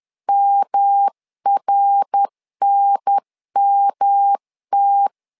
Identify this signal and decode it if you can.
Morse code, heard as a keyed tone, about 10 wpm, 800 Hz. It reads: MRNMT
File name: LEZIONI / 3 MWRO / 3 MWR 10wpm 1952